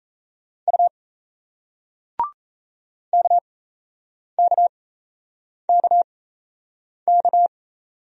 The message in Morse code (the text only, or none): U KKKK